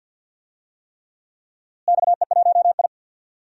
Morse code, heard as a keyed tone, about 49 wpm, 700 Hz. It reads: XE1I
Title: DXCC-100 Call Signs - 4 Characters - Repeated 3x - 50wpm